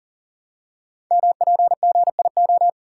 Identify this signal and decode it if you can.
Morse code, heard as a keyed tone, about 39 wpm, 690 Hz. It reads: MPGIO
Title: Code Groups - 5 Letters - 40wpm